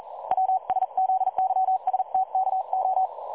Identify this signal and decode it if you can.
Morse code, heard as a keyed tone, about 43 wpm, 760 Hz. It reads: WSZYSTKO
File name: dlazaawansowanych / nagraniazpasma / websdr recording 3565.0kHz